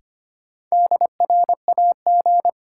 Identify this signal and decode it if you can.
Morse code, heard as a keyed tone, about 25 wpm, 700 Hz. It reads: DRAG
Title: Sentences from Top 1500 Words - New Word Drill - 25wpm